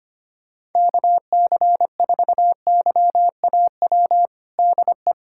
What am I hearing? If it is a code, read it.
Morse code, heard as a keyed tone, about 25 wpm, 700 Hz. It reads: KC4YAW DE
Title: USA Callsign DE Callsign 25wpm